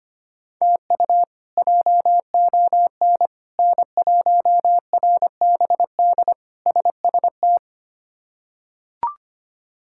Morse code, heard as a keyed tone, about 25 wpm, 700 Hz. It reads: TU JON N1RBD SST
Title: K1USN SST QSO Practice - 25wpm